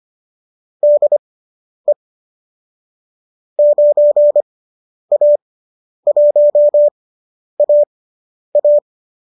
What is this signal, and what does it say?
Morse code, heard as a keyed tone, about 25 wpm, spaced as for 10 wpm, 590 Hz. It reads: DE 9A1AA
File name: qso219 10